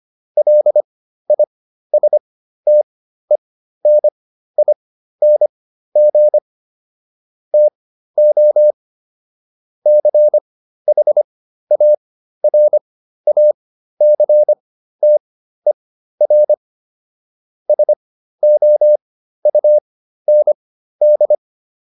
Morse code, heard as a keyed tone, about 25 wpm, spaced as for 13 wpm, 600 Hz. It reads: LISTENING TO CHARACTER SOUND